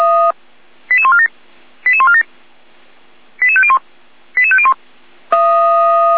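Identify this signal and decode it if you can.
Five-tone selective call (ZVEI), heard as quick runs of five tones, tones of about 70 ms. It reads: 88147 88147 88501 88501
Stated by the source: ZVEI five-tone selective call 88397 received by SDR.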